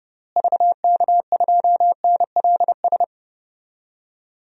Morse code, heard as a keyed tone, about 30 wpm, 700 Hz. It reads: VK2NLS
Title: DXCC-100 Call Signs - 6 Characters - Repeated 3x - 30wpm